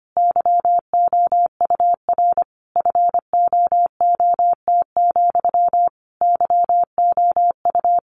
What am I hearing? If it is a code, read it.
Morse code, heard as a keyed tone, about 25 wpm, 700 Hz. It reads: YOUR FOOT, YOU
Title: A Sound Of Thunder 25wpm